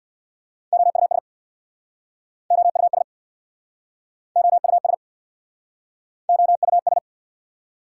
Morse code, heard as a keyed tone, about 69 wpm, 700 Hz. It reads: OUS OUS OUS OUS